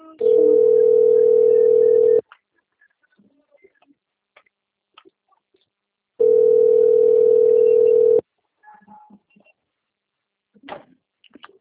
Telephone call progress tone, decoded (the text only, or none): ringback tone